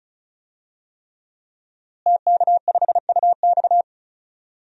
Morse code, heard as a keyed tone, about 36 wpm, 700 Hz. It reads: TK5UX
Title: DXCC-100 Call Signs - 5 Characters - Repeated 3x - 35wpm